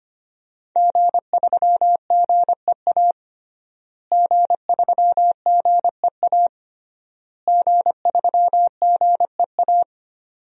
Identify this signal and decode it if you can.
Morse code, heard as a keyed tone, about 25 wpm, 700 Hz. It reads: G3GEA G3GEA G3GEA
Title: DXCC-100 Call Signs - 5 Characters - Repeated 3x - 25wpm